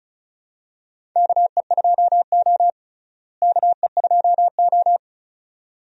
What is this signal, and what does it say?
Morse code, heard as a keyed tone, about 34 wpm, 700 Hz. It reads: KE2O KE2O
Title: USA Call Signs - 4 Characters - Repeated 2x - 35wpm